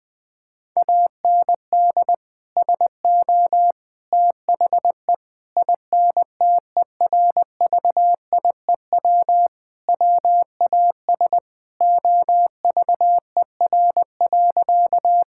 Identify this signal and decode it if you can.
Morse code, heard as a keyed tone, about 21 wpm, 700 Hz. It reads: AND SO THE INTERVIEW WAS OVER.
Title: Sentences from Top 1600 Words 20wpm